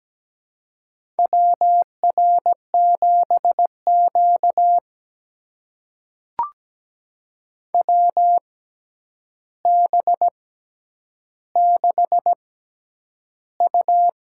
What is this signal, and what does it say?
Morse code, heard as a keyed tone, about 17 wpm, spaced as for 9 wpm, 700 Hz. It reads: WR7Q WB6U
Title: USA Call Signs - 6x Character Spacing - 17wpm